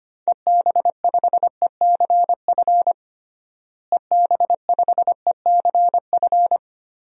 Morse code, heard as a keyed tone, about 25 wpm, 700 Hz. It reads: EB5ECF EB5ECF
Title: DXCC-100 Call Signs - 6 Characters - Repeated 2x - 25wpm